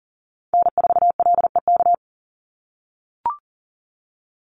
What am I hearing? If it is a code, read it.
Morse code, heard as a keyed tone, about 40 wpm, 700 Hz. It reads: N4LEK
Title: USA Call Signs 40wpm